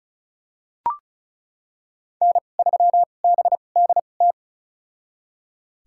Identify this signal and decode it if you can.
Morse code, heard as a keyed tone, about 34 wpm, spaced as for 25 wpm, 700 Hz. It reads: N3BDT
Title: USA Call Signs - 5 Characters - 2x Character Spacing - 35wpm